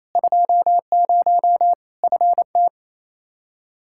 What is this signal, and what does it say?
Morse code, heard as a keyed tone, about 28 wpm, 700 Hz. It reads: 20 FT
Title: Practice QSO - Exchange 2 - Info - 28wpm